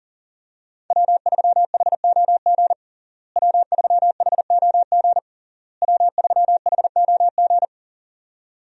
Morse code, heard as a keyed tone, about 40 wpm, 700 Hz. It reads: W3HOG W3HOG W3HOG
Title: USA Call Signs - 5 Characters - Repeated 3x - 40wpm